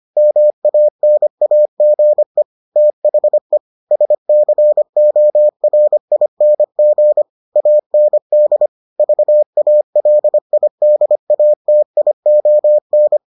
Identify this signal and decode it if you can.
Morse code, heard as a keyed tone, about 25 wpm, 600 Hz. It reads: MANAGE THE SCORING AND VALIDATION